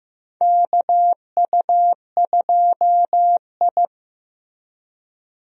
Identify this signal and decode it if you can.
Morse code, heard as a keyed tone, about 15 wpm, 700 Hz. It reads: KU2I
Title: USA Call Signs - 4 Characters - Repeated 2x - 15wpm